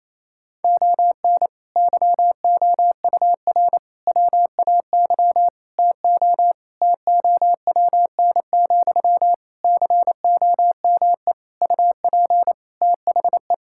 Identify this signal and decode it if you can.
Morse code, heard as a keyed tone, about 28 wpm, 700 Hz. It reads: ON YOUR WAY TO TOWN, COME UP THE
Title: Sentences from Top 1000 Words - Review Part 6 - 28wpm